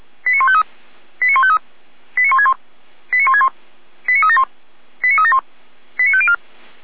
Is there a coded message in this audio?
Yes: five-tone selective call (ZVEI) from 0.3 to 6.3 s, heard as quick runs of five tones, tones of about 70 ms. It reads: 89153 89153 89161 89161 89371 89371 89504